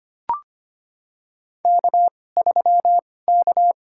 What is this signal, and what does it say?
Morse code, heard as a keyed tone, about 25 wpm, spaced as for 18 wpm, 700 Hz. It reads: K3K
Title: USA Call Signs - 2x Character Spacing - 25wpm